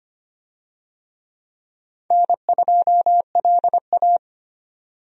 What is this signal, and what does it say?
Morse code, heard as a keyed tone, about 25 wpm, 700 Hz. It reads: N2LA
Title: USA Call Signs - 4 Characters - 25wpm